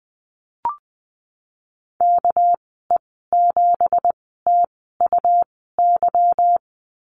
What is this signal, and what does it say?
Morse code, heard as a keyed tone, about 20 wpm, spaced as for 14 wpm, 700 Hz. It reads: KE7TUY
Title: 213 Lesson 34g - Number 8 - Call Signs 20wpm